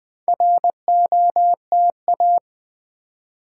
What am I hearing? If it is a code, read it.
Morse code, heard as a keyed tone, about 20 wpm, 700 Hz. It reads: ROTA
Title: Build A Word - 4 Letters At A Time - 8-Letter Words - 20wpm